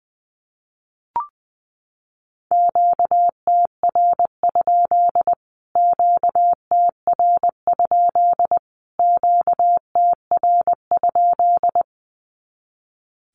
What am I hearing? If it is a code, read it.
Morse code, heard as a keyed tone, about 20 wpm, 700 Hz. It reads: QTR? QTR? QTR?
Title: Q-Codes Repeated 3x 20wpm